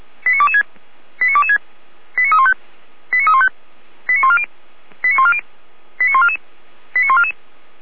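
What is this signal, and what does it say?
Five-tone selective call (ZVEI), heard as quick runs of five tones, tones of about 80 ms. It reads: 89226 89226 89315 89315 89140 89140 89144 89144
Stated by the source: ZVEI five-tone call received by SDR, logged by Feuerwehr Orlamünde (f85360).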